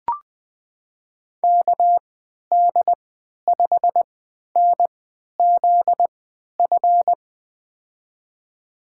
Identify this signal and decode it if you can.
Morse code, heard as a keyed tone, about 20 wpm, spaced as for 11 wpm, 700 Hz. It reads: KD5NZF
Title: USA Call Signs - 6 Characters - 3x Character Spacing - 20wpm